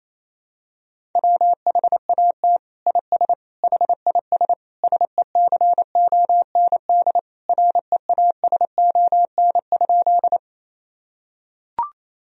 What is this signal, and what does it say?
Morse code, heard as a keyed tone, about 28 wpm, 700 Hz. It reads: WHAT IS HIS SECOND REASON?
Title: Sentences from Top 1000 Words - Review Part 1 - 28wpm